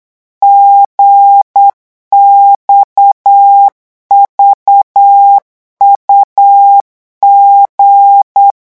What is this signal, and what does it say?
Morse code, heard as a keyed tone, about 8 wpm, 800 Hz. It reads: GXVUG
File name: morse mp3 / revision3-vitesse 8 mots